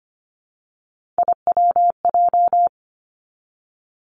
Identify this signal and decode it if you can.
Morse code, heard as a keyed tone, about 25 wpm, 700 Hz. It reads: IWJ
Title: Three Challenging Letter-Number Combinations 25wpm